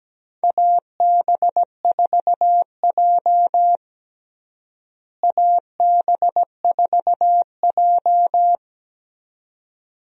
Morse code, heard as a keyed tone, about 17 wpm, 700 Hz. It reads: AB4J AB4J
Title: USA Call Signs - 4 Characters - Repeated 2x - 17wpm